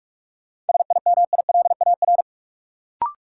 Morse code, heard as a keyed tone, about 44 wpm, 700 Hz. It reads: SIMILAR
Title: Sentences from Top 1000 Words - Review Part 5 - New Word Drill - 45wpm